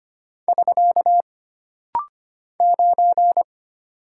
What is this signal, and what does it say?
Morse code, heard as a keyed tone, about 25 wpm, 700 Hz. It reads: <SK> 9
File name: Single Character (Letter:Number:Punctuation:Prosign) - Rapid-Fire - 25wpm